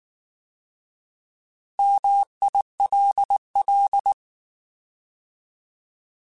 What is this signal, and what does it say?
Morse code, heard as a keyed tone, about 19 wpm, 800 Hz. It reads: MILL
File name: Alphabetical / Alphabetical mp3s / Ms @18